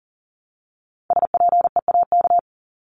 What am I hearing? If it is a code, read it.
Morse code, heard as a keyed tone, about 40 wpm, 700 Hz. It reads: SPEAK
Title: ING Suffix - Top 100 Words with Base Words 40wpm